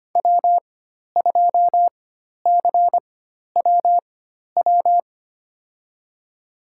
Morse code, heard as a keyed tone, about 25 wpm, spaced as for 12 wpm, 700 Hz. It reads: W2CWW
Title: USA Call Signs - 5 Characters - 4x Character Spacing - 25wpm